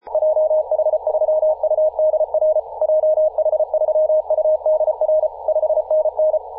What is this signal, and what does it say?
Morse code, heard as a keyed tone, about 34 wpm, 620 Hz. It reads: JH3UDR JH3UDR 5NN
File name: posts / 2017-aug-07-0652z-ri0li-14-036mhz-cw